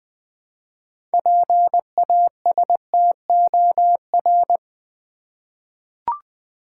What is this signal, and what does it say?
Morse code, heard as a keyed tone, about 20 wpm, 700 Hz. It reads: PASTOR
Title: 6 Letter Words - First 3 Letters Given - 20wpm